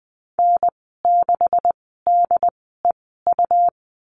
Morse code, heard as a keyed tone, about 20 wpm, spaced as for 15 wpm, 700 Hz. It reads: N6DEU
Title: 167 Lesson 27g - Number 7 - Call Signs 20wpm